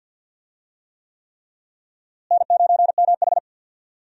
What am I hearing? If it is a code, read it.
Morse code, heard as a keyed tone, about 51 wpm, 700 Hz. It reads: N9MH
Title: USA Call Signs - 4 Characters - Repeated 3x - 50wpm